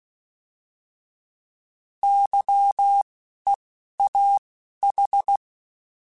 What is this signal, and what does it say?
Morse code, heard as a keyed tone, about 16 wpm, 800 Hz. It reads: Y E A H
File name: Misc / Misc-mp3s / Misc @10